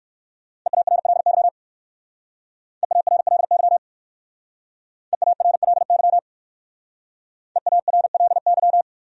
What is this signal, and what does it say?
Morse code, heard as a keyed tone, about 56 wpm, 700 Hz. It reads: EARLY EARLY EARLY EARLY
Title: Speed-Racing Top 500 Words 45wpm